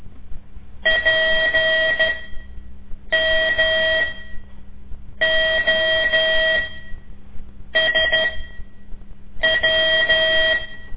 Morse code, heard as a keyed tone, about 13 wpm, spaced as for 6 wpm, 640 Hz, with noise. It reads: PMOSW
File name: ik1weg2 / CORSI CW / lez5-2